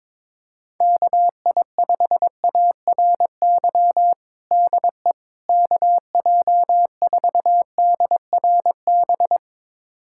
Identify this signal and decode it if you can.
Morse code, heard as a keyed tone, about 22 wpm, 700 Hz. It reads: KI5ARY DE KJ4DRB
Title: USA Callsign DE Callsign 22wpm